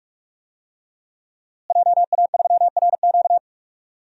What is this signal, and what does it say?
Morse code, heard as a keyed tone, about 41 wpm, 700 Hz. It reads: JA3RQ
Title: DXCC-100 Call Signs - 5 Characters - Repeated 3x - 45wpm